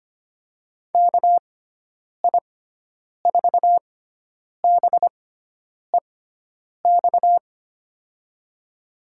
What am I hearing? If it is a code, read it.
Morse code, heard as a keyed tone, about 25 wpm, spaced as for 9 wpm, 700 Hz. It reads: KI4BEX
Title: USA Call Signs - 6x Character Spacing - 25wpm